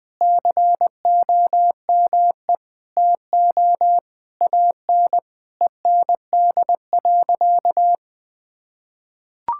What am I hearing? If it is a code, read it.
Morse code, heard as a keyed tone, about 20 wpm, 700 Hz. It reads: COME TO AN END.
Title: Sentences from Top 1000 Words - Review Part 5 - 20wpm